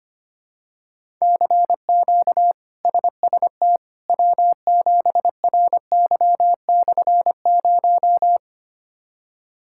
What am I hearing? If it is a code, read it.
Morse code, heard as a keyed tone, about 25 wpm, 700 Hz. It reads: CQ SST W7RY/0
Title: K1USN SST QSO Practice - 25wpm